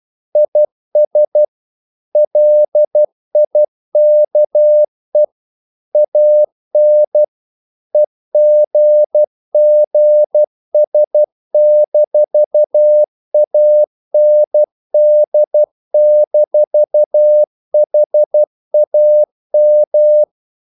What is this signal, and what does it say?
Morse code, heard as a keyed tone, about 12 wpm, 600 Hz. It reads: IS LIKE AN EGGS-AND-HAM
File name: CW / Practice / EnglishQuotations0112